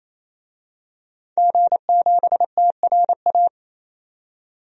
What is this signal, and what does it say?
Morse code, heard as a keyed tone, about 28 wpm, 700 Hz. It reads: G7TRA
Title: DXCC-100 Call Signs - 5 Characters - 28wpm